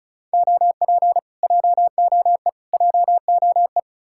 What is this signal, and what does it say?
Morse code, heard as a keyed tone, about 48 wpm, spaced as for 42 wpm, 700 Hz. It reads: OP JOE JOE
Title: Speed-Racing Common QSO Phrases - 35wpm